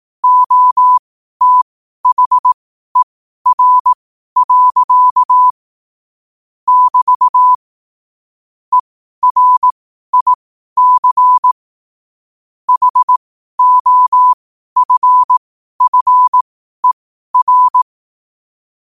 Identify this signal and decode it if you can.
Morse code, heard as a keyed tone, about 18 wpm, spaced as for 12 wpm, 1000 Hz. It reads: OTHER. <BT> ERIC HOFFER
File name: qotdfiles / 12wpm